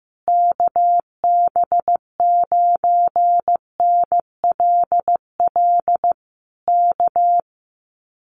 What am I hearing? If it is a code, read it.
Morse code, heard as a keyed tone, about 15 wpm, 700 Hz. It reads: KB9NLL K
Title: Common QSO Phrases Encore 15wpm